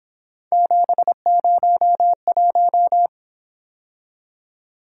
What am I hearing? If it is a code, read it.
Morse code, heard as a keyed tone, about 26 wpm, 700 Hz. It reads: 701
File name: Speed-Racing - Three Letter-Number Combinations - 22wpm